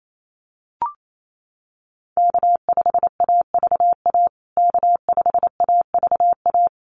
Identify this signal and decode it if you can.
Morse code, heard as a keyed tone, about 28 wpm, 700 Hz. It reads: K5AVA K5AVA
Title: USA Call Signs Repeated 3x 28wpm